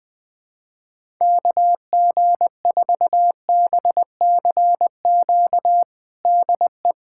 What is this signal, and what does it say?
Morse code, heard as a keyed tone, about 20 wpm, 700 Hz. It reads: KG4BCQ DE
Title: USA Callsign DE Callsign - 6 Characters - 20wpm